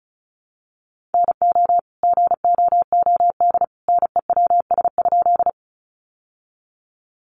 Morse code, heard as a keyed tone, about 34 wpm, 700 Hz. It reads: NO GOOD NEWS?
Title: Sentences from Top 800 Words 35wpm